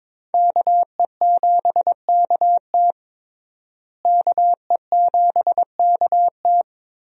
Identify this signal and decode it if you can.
Morse code, heard as a keyed tone, about 22 wpm, 700 Hz. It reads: KE7KT KE7KT
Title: USA Call Signs - 5 Characters - Repeated 3x - 22wpm